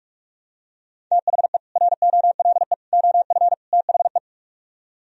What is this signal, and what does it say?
Morse code, heard as a keyed tone, about 49 wpm, 700 Hz. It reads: THE ROLE OF THE